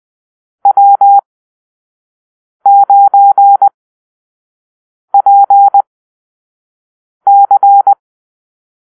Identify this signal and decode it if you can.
Morse code, heard as a keyed tone, about 20 wpm, spaced as for 5 wpm, 800 Hz. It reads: W9PC
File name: arrl / Call-Sign Drill-1